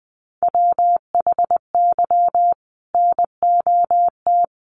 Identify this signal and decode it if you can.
Morse code, heard as a keyed tone, about 20 wpm, 700 Hz. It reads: WHY NOT
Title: Sentences from Top 1500 Words 20wpm